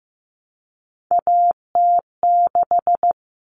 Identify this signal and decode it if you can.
Morse code, heard as a keyed tone, about 15 wpm, 700 Hz. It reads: AT6
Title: Common Cut Numbers - 3-Digit 15wpm